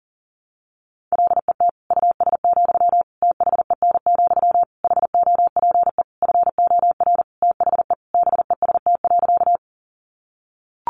Morse code, heard as a keyed tone, about 40 wpm, 700 Hz. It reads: LET US, THEN, HOPE FOR THE BEST.
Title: Sentences from Top 1000 Words 40wpm